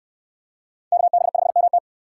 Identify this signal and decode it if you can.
Morse code, heard as a keyed tone, about 62 wpm, 700 Hz. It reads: KB5MT